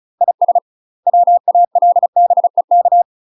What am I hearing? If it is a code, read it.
Morse code, heard as a keyed tone, about 34 wpm, 700 Hz. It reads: IS WALBEK
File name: UK MASTER QSOs TRAINING / 35WPM / UK MASTER QSOs 0001